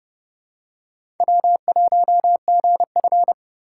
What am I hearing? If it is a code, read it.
Morse code, heard as a keyed tone, about 30 wpm, 700 Hz. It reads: W1GF
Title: USA Call Signs - 3x Character Spacing - 30wpm